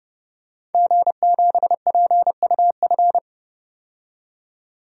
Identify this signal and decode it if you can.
Morse code, heard as a keyed tone, about 32 wpm, 700 Hz. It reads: G7PUF
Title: DXCC-100 Call Signs - 5 Characters - 30wpm